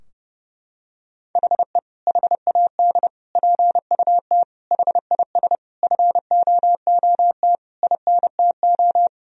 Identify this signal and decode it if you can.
Morse code, heard as a keyed tone, about 32 wpm, 700 Hz. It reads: HE HAD PUT HIS FOOT INTO
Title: Sentences from Top 1000 Words - Review Part 3 - 30wpm